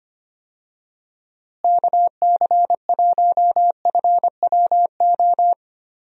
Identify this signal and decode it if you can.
Morse code, heard as a keyed tone, about 25 wpm, 700 Hz. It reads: KC1FWO
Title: USA Call Signs - 6 Characters - Repeated 3x - 25wpm